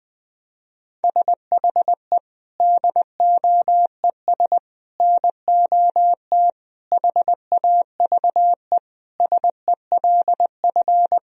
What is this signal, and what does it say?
Morse code, heard as a keyed tone, about 21 wpm, 700 Hz. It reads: SHE DOES NOT HAVE SELF
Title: Sentences from Top 1000 Words - Review Part 3 - 20wpm